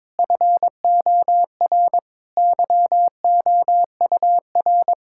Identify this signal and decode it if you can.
Morse code, heard as a keyed tone, about 22 wpm, 700 Hz. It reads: FOR YOUR
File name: Difficult Sentences with Hints - 22wpm